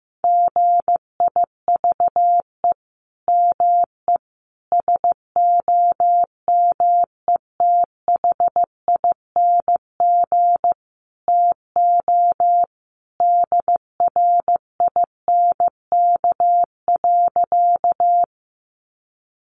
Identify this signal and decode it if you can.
Morse code, heard as a keyed tone, about 15 wpm, 700 Hz. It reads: GIVE ME SOMETHING TO DRINK.
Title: Sentences from Top 1500 Words 15wpm